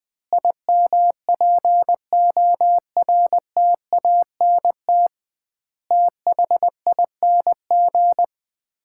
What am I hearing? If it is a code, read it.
Morse code, heard as a keyed tone, about 20 wpm, 700 Hz. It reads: IMPORTANT THING
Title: 2x Word-Spacing - Sets of 4 Words - 20wpm